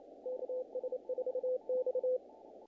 Morse code, heard as a keyed tone, about 29 wpm, 500 Hz. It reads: KS4X
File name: contests / naqp-cw-k5tr-op-k5pi- -1800